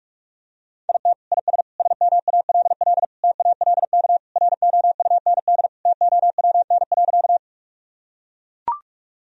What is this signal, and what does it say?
Morse code, heard as a keyed tone, about 43 wpm, 700 Hz. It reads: IT IS SMALL TALK ROUND TOWN.
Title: Sentences from Top 1000 Words - Review Part 1 - 45wpm